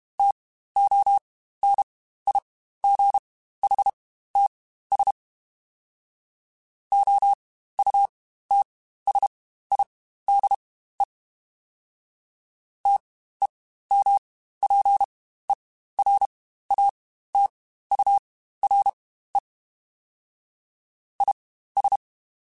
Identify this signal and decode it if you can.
Morse code, heard as a keyed tone, about 32 wpm, spaced as for 15 wpm, 800 Hz. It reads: TONIGHTS OUTSIDE TEMPERATURE IS
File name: Spaced Characters / Spaced Characters-mp3s / Weather5 1 sp @30